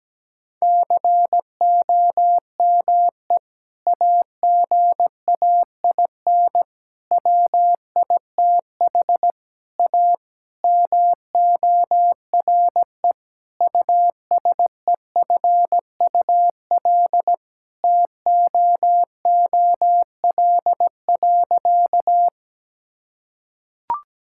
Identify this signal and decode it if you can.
Morse code, heard as a keyed tone, about 17 wpm, 700 Hz. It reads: COME AGAIN WITH A MORE USEFUL TOOL.
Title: Sentences from Top 1000 Words - Review Part 6 - 17wpm